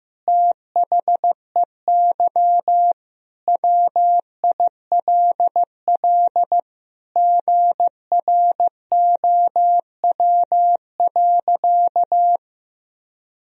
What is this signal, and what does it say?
Morse code, heard as a keyed tone, about 15 wpm, 700 Hz. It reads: THEY WILL GROW.
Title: Sentences from Top 1000 Words - Review Part 1 - 15wpm